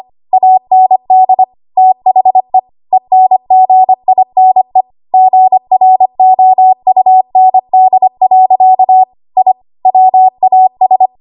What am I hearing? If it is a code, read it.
Morse code, heard as a keyed tone, about 25 wpm, 750 Hz. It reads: AND THE ENGINE GROUND. I WAS
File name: Archive / 25 WPM / 25WPM